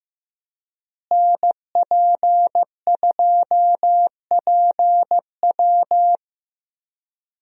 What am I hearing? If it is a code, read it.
Morse code, heard as a keyed tone, about 15 wpm, 700 Hz. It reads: NP2PW
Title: USA Call Signs - 5 Characters - Repeated 3x - 15wpm